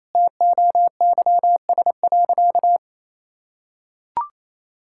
Morse code, heard as a keyed tone, about 28 wpm, 700 Hz. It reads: TOYS.
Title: Pangram Challenge v2 - 28wpm